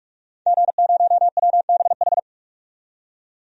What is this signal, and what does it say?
Morse code, heard as a keyed tone, about 46 wpm, 700 Hz. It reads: G0WBH
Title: DXCC-100 Call Signs - 5 Characters - 45wpm